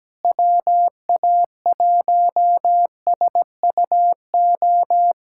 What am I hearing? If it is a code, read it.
Morse code, heard as a keyed tone, about 17 wpm, 700 Hz. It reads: WA1SUO
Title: USA Callsign DE Callsign - 6 Characters - 17wpm